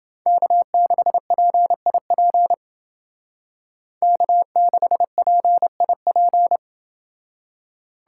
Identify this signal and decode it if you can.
Morse code, heard as a keyed tone, about 29 wpm, 700 Hz. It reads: K6PIP K6PIP